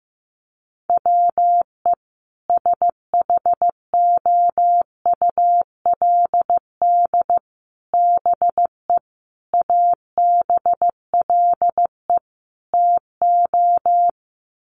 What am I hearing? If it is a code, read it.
Morse code, heard as a keyed tone, about 15 wpm, 700 Hz. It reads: WE SHOULD BE ABLE TO
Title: Sets of 5 Words Encore2 15wpm